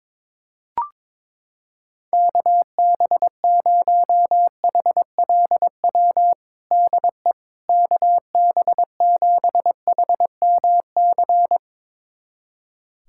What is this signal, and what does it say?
Morse code, heard as a keyed tone, about 23 wpm, 700 Hz. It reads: KB0HLW DE KB7HMC
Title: USA Callsign DE Callsign - 6 Characters - 22wpm